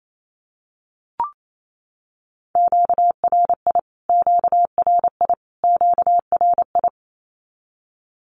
Morse code, heard as a keyed tone, about 28 wpm, 700 Hz. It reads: QRI QRI QRI
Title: Q-Codes Repeated 3x 28wpm